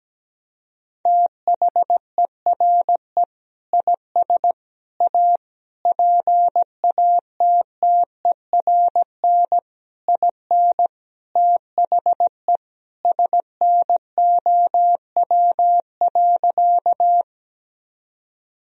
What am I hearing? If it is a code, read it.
Morse code, heard as a keyed tone, about 17 wpm, 700 Hz. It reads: THERE IS A PATTERN IN THE SNOW.
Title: Sentences from Top 1000 Words - Review Part 2 - 17wpm